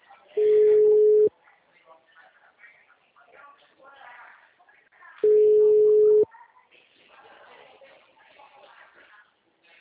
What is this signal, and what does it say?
Ringback tone (Europe, 425 Hz), with speech.